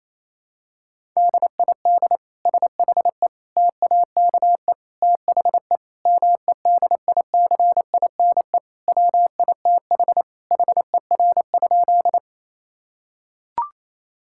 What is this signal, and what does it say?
Morse code, heard as a keyed tone, about 28 wpm, 700 Hz. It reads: DID SHE TAKE THE MEDICINE WITH HER?
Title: Sentences from Top 1000 Words - Review Part 6 - 28wpm